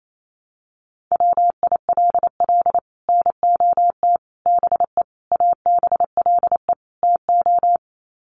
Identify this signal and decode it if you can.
Morse code, heard as a keyed tone, about 28 wpm, 700 Hz. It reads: WILL NOT BE ABLE TO
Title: Sets of 5 Words 28wpm